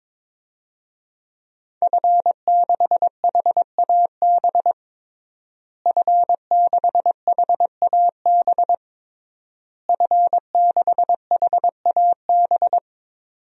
Morse code, heard as a keyed tone, about 22 wpm, 700 Hz. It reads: F6HAB F6HAB F6HAB
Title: DXCC-100 Call Signs - 5 Characters - Repeated 3x - 22wpm